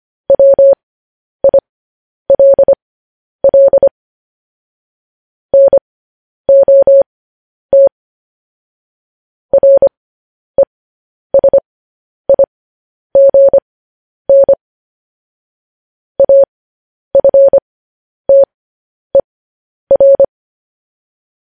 Morse code, heard as a keyed tone, about 25 wpm, spaced as for 10 wpm, 550 Hz. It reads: WILL NOT RESIGN AFTER